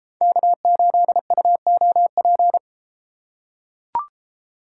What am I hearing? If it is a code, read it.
Morse code, heard as a keyed tone, about 31 wpm, 700 Hz. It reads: K8UOP